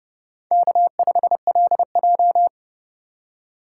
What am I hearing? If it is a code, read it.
Morse code, heard as a keyed tone, about 29 wpm, 700 Hz. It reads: K5LJ